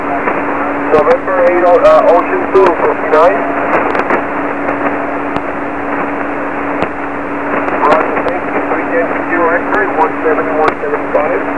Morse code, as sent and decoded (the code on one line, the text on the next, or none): none